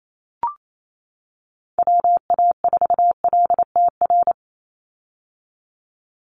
Morse code, heard as a keyed tone, about 28 wpm, 700 Hz. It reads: WA4LTR
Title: USA Call Signs 28wpm